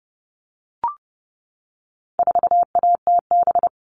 Morse code, heard as a keyed tone, about 30 wpm, 700 Hz. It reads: 4ATB
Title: All Cut Numbers - 4-Digit 30wpm